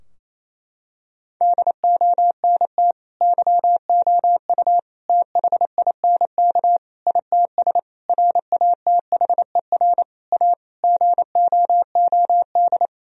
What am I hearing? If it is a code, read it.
Morse code, heard as a keyed tone, about 28 wpm, 700 Hz. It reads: DONT YOU THINK ITS RATHER A GOOD
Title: Sentences from Top 1000 Words - Review Part 4 - 28wpm